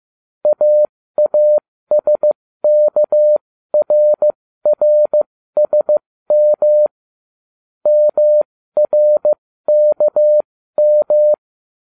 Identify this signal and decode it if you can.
Morse code, heard as a keyed tone, about 15 wpm, spaced as for 13 wpm, 600 Hz. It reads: AASKRRSM MRKM